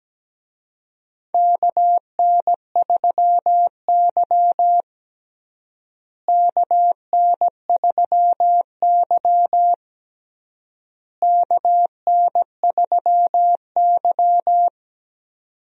Morse code, heard as a keyed tone, about 17 wpm, 700 Hz. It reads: KN3Y KN3Y KN3Y
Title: USA Call Signs - 4 Characters - Repeated 3x - 17wpm